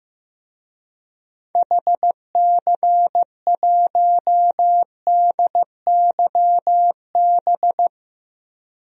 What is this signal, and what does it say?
Morse code, heard as a keyed tone, about 15 wpm, 700 Hz. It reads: HC1DYB
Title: DXCC-100 Call Signs - 6 Characters - Repeated 2x - 15wpm